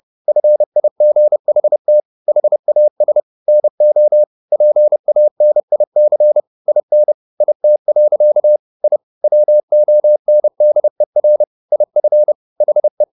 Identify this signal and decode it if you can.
Morse code, heard as a keyed tone, about 30 wpm, 600 Hz. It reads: FIGHT HAS NO PANIC IN IT. I WONDER IF HE
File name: the old man and the sea / the old man and the sea 1 30wpm